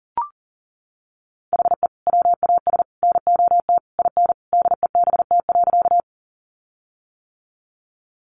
Morse code, heard as a keyed tone, about 40 wpm, 700 Hz. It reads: HE WAS NOT IN DEBT.
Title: Sentences from Top 1300 Words 40wpm